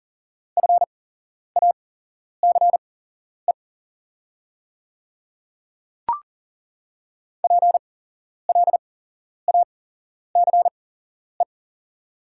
Morse code, heard as a keyed tone, about 40 wpm, spaced as for 11 wpm, 690 Hz. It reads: FACE PLACE
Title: 8x Character Spacing - Top 500 Words - 40wpm